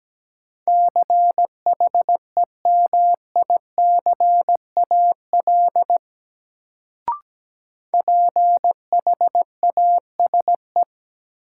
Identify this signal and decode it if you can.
Morse code, heard as a keyed tone, about 17 wpm, 700 Hz. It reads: CHEMICAL PHASE
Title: Sentences from Top 1200 Words - New Word Drill - 17wpm